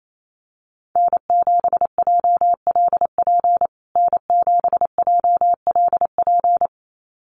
Morse code, heard as a keyed tone, about 28 wpm, 700 Hz. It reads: N7JLP N7JLP
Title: USA Call Signs Repeated 2x 28wpm